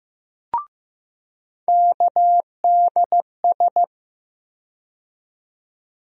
Morse code, heard as a keyed tone, about 15 wpm, 700 Hz. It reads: KDS
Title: USA Call Sign Postfixes - 3 Letters 15wpm